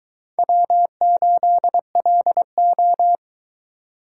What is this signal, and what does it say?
Morse code, heard as a keyed tone, about 20 wpm, 700 Hz. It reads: W8LO